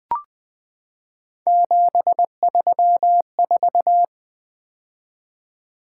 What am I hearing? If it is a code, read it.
Morse code, heard as a keyed tone, about 20 wpm, 700 Hz. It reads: 734
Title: Three Letter-Number Combinations - 20wpm